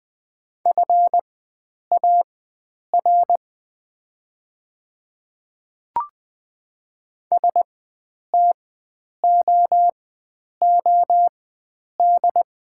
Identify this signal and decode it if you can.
Morse code, heard as a keyed tone, about 20 wpm, spaced as for 9 wpm, 700 Hz. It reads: FAR STOOD
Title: 4x Character Spacing - Top 500 Words - 20wpm